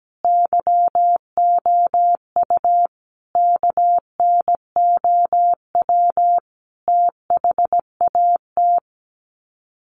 Morse code, heard as a keyed tone, about 17 wpm, 700 Hz. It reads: YOU KNOW THAT
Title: Sets of 5 Words Encore 17wpm